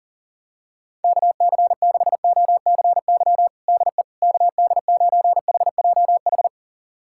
Morse code, heard as a keyed tone, about 40 wpm, 700 Hz. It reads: KC6OCY DE KD9HJH
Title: USA Callsign DE Callsign - 6 Characters - 40wpm